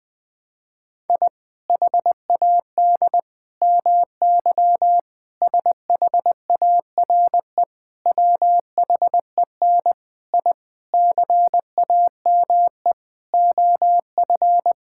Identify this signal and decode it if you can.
Morse code, heard as a keyed tone, about 19 wpm, 700 Hz. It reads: I HAD MY SHARE WHEN I CAME OF
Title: Sentences from Top 1000 Words - Review Part 6 - 20wpm